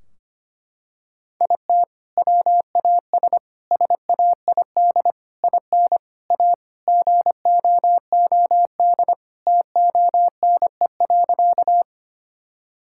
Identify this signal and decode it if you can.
Morse code, heard as a keyed tone, about 25 wpm, 700 Hz. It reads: IT WAS SAID IN A GOOD TONE.
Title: Sentences from Top 1600 Words 25wpm